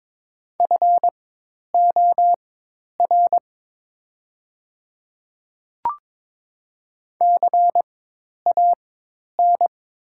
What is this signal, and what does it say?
Morse code, heard as a keyed tone, about 22 wpm, spaced as for 10 wpm, 700 Hz. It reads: FOR CAN
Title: 4x Character Spacing - Top 200 Words - 22wpm